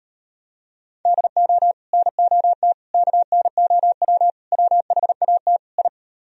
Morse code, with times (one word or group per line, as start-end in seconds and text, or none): 1.1-1.7 DO
1.9-2.7 NOT
2.9-4.3 KNOW
4.5-5.6 WHAT
5.8-5.9 I